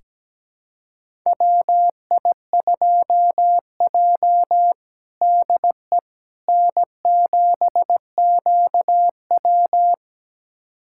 Morse code, heard as a keyed tone, about 17 wpm, 700 Hz. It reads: WI2J DE N7QW
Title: USA Callsign DE Callsign - 4 Characters - 17wpm